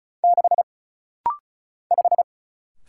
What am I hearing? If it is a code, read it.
Morse code, heard as a keyed tone, about 36 wpm, 700 Hz. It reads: 6 5
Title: Single Letter-Number - Rapid-Fire 35wpm